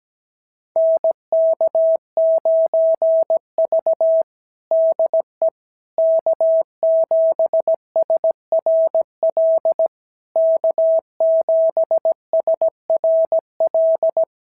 Morse code, heard as a keyed tone, about 16 wpm, 650 Hz. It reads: NK9V DE K7SRL K7SRL
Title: SOTA QSO Practice 17wpm